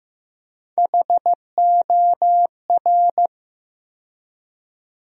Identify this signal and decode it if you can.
Morse code, heard as a keyed tone, about 15 wpm, 700 Hz. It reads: HOR
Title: USA Call Sign Postfixes - 3 Letters 15wpm